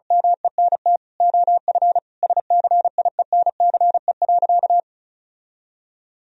Morse code, heard as a keyed tone, about 36 wpm, 700 Hz. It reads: MENT OF SCIENCE.
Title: Sentences from Top 1000 Words - Review Part 4 - 35wpm